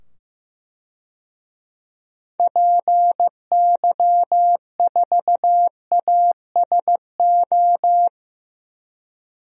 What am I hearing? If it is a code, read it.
Morse code, heard as a keyed tone, about 15 wpm, 700 Hz. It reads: PY4ASO
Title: DXCC-100 Call Signs - 6 Characters - 15wpm